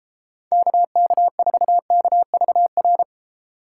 Morse code, heard as a keyed tone, about 32 wpm, 700 Hz. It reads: KK4KVR